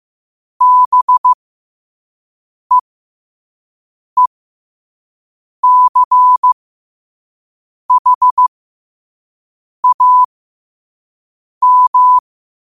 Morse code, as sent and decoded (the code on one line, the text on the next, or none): -... . -.-. .... .- --
BEECHAM